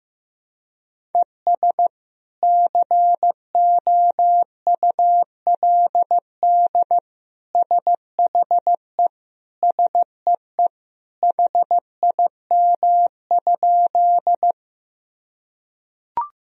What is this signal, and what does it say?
Morse code, heard as a keyed tone, about 15 wpm, 700 Hz. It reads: ES COULD SHE SEE HIM?
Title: Sentences from Top 100 Words and Abbreviations 15wpm